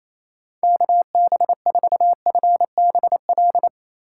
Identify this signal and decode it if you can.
Morse code, heard as a keyed tone, about 28 wpm, 700 Hz. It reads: KB4FBL
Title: USA Call Signs - 6 Characters - 3x Character Spacing - 28wpm